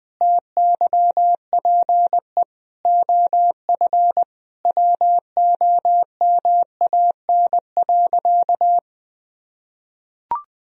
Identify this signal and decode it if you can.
Morse code, heard as a keyed tone, about 20 wpm, 700 Hz. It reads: TYPE OF WOMAN.
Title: Sentences from Top 1000 Words - Review Part 3 - 20wpm